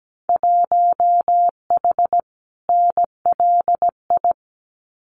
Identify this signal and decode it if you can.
Morse code, heard as a keyed tone, about 17 wpm, 700 Hz. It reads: 1H NLI
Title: Winter Field Day Exchanges 17wpm